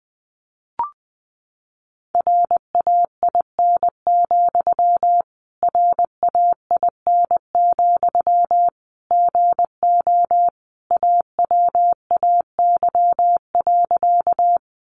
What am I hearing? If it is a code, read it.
Morse code, heard as a keyed tone, about 20 wpm, 700 Hz. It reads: RAIN, RAIN, GO AWAY.
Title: Sentences from Top 600 Words 20wpm